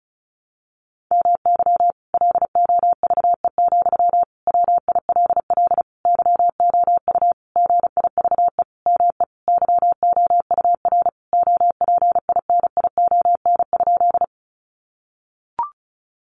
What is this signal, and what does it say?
Morse code, heard as a keyed tone, about 34 wpm, 700 Hz. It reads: MY LOVE, WILL YOU GIVE ME YOUR OPINION?
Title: Sentences from Top 1200 Words 35wpm